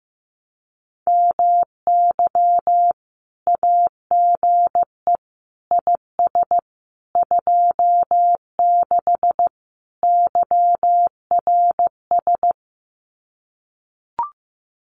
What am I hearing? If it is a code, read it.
Morse code, heard as a keyed tone, about 15 wpm, 700 Hz. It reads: MY AGE IS 26 YRS
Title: Common QSO Phrases Encore 15wpm